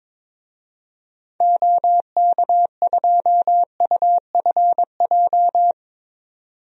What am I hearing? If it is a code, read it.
Morse code, heard as a keyed tone, about 22 wpm, 700 Hz. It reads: OK2UFJ
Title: DXCC-100 Call Signs - 6 Characters - 22wpm